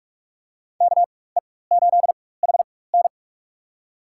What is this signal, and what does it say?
Morse code, heard as a keyed tone, about 42 wpm, spaced as for 21 wpm, 700 Hz. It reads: KE8HN
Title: USA Call Signs - 4x Character Spacing - 45wpm